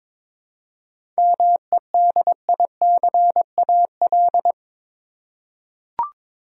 Morse code, heard as a keyed tone, about 22 wpm, 700 Hz. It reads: MEDICAL
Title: Seven Letter Words - 22wpm